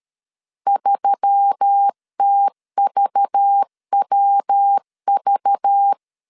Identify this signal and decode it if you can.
Morse code, heard as a keyed tone, about 13 wpm, 800 Hz. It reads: 3TVWV
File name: LEZIONI / 6 PKX0 / 6 PKX 12wpm 2024